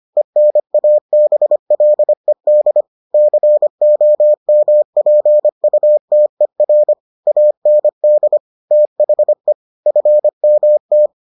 Morse code, heard as a keyed tone, about 25 wpm, 600 Hz. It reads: ENABLED COMPUTER AND THE FMT